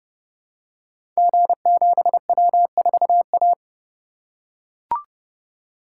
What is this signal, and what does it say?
Morse code, heard as a keyed tone, about 28 wpm, 700 Hz. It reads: G7W4A
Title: Code Groups - 5 Letter:Numbers - 30wpm